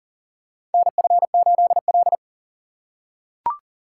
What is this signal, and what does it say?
Morse code, heard as a keyed tone, about 40 wpm, 700 Hz. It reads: NF8L
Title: USA Call Signs - 6x Character Spacing - 40wpm